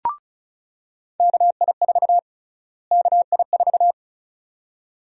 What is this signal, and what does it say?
Morse code, heard as a keyed tone, about 36 wpm, 700 Hz. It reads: KI4 KI4
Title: USA Call Sign Prefixes - Repeated 2x - 35wpm